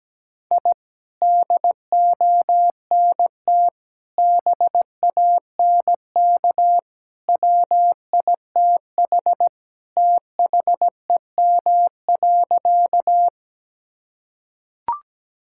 Morse code, heard as a keyed tone, about 17 wpm, 700 Hz. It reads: I DONT BANK WITH THEM.
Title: Sentences from Top 1000 Words - Review Part 6 - 17wpm